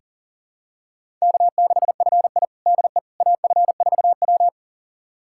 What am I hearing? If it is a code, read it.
Morse code, heard as a keyed tone, about 40 wpm, 700 Hz. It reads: K6FI DE AF4W
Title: USA Callsign DE Callsign - 4 Characters - 40wpm